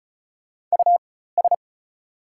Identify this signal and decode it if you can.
Morse code, heard as a keyed tone, about 36 wpm, spaced as for 16 wpm, 700 Hz. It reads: US
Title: 4x Character Spacing - Top 300 Words - 35wpm